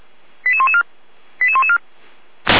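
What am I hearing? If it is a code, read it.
Five-tone selective call (ZVEI), heard as quick runs of five tones, tones of about 70 ms. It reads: 88114 88114
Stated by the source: ZVEI five-tone selective call received on SDR by feuerwehr-orlamuende.org, fire service.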